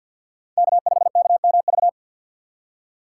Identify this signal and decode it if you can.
Morse code, heard as a keyed tone, about 51 wpm, 700 Hz. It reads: K5KMV
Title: USA Call Signs - 5 Characters - 3x Character Spacing - 50wpm